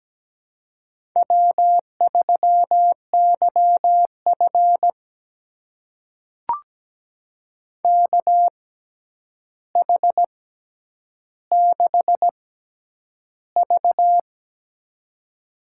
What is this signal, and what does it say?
Morse code, heard as a keyed tone, about 17 wpm, spaced as for 9 wpm, 700 Hz. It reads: W3YF KH6V
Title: USA Call Signs - 4 Characters - 6x Character Spacing - 17wpm